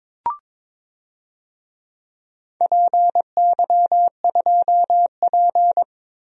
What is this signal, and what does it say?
Morse code, heard as a keyed tone, about 22 wpm, 700 Hz. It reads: PY2P
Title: DXCC-100 Call Signs - 4 Characters - 22wpm